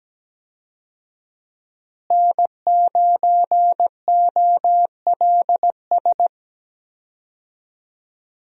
Morse code, heard as a keyed tone, about 17 wpm, 700 Hz. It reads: N9OLS
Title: USA Call Signs - 5 Characters - Repeated 2x - 17wpm